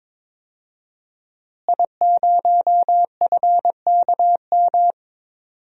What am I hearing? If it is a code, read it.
Morse code, heard as a keyed tone, about 22 wpm, 700 Hz. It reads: I0FKM
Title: DXCC-100 Call Signs - 5 Characters - Repeated 3x - 22wpm